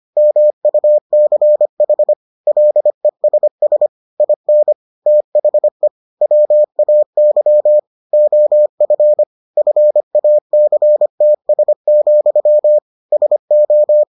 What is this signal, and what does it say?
Morse code, heard as a keyed tone, about 25 wpm, 600 Hz. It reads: MUCH LESS IN THE WAY OF FACTS, SO